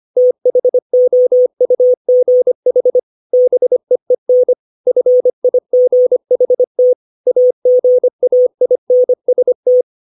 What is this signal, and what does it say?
Morse code, heard as a keyed tone, about 25 wpm, 500 Hz. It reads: THOUGH BEEN FIGHT AGAINST
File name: 632 words / 632 Words 25WPM